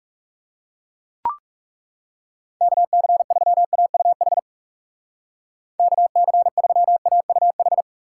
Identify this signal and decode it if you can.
Morse code, heard as a keyed tone, about 40 wpm, 700 Hz. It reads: KC3AUH KC3AUH